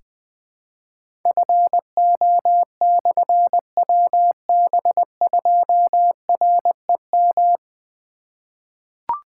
Morse code, heard as a keyed tone, about 20 wpm, 700 Hz. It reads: FO/WB2REM
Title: Difficult Call Signs 20wpm